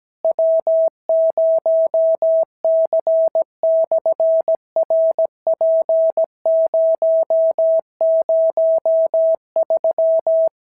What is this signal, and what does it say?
Morse code, heard as a keyed tone, about 17 wpm, 650 Hz. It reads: W0C/RP003
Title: SOTA QSO Practice 17wpm